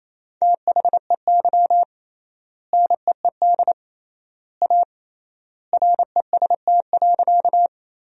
Morse code, heard as a keyed tone, about 27 wpm, 700 Hz. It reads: THEY NEED A REST.
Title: 3x Word-Spacing - Sentences from Top 500 Words - 28wpm